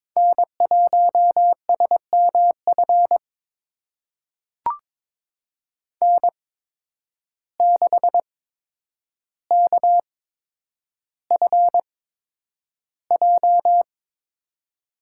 Morse code, heard as a keyed tone, about 22 wpm, spaced as for 9 wpm, 700 Hz. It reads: N1SMF N6KFJ
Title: USA Call Signs - 5 Characters - 8x Character Spacing - 22wpm